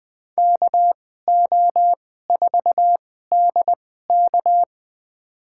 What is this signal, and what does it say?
Morse code, heard as a keyed tone, about 20 wpm, spaced as for 15 wpm, 700 Hz. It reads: KO4DK
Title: USA Call Signs - 5 Characters - 2x Character Spacing - 20wpm